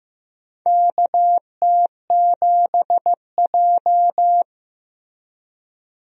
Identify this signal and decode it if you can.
Morse code, heard as a keyed tone, about 15 wpm, 700 Hz. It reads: KT7J
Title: USA Call Signs - 4 Characters - Repeated 2x - 15wpm